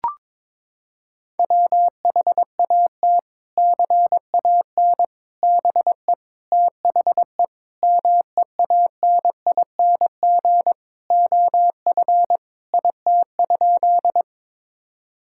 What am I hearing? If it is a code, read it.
Morse code, heard as a keyed tone, about 22 wpm, 700 Hz. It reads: WHAT CAN BE THE MEANING OF IT?
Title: Sentences from Top 1000 Words - Review Part 6 - 22wpm